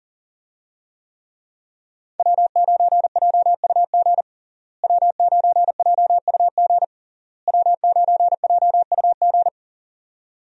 Morse code, heard as a keyed tone, about 40 wpm, 700 Hz. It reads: W9JUG W9JUG W9JUG
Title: USA Call Signs - 5 Characters - Repeated 3x - 40wpm